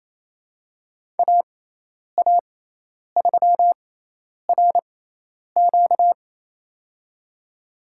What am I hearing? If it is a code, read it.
Morse code, heard as a keyed tone, about 28 wpm, spaced as for 10 wpm, 700 Hz. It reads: AA3RQ
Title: USA Call Signs - 6x Character Spacing - 28wpm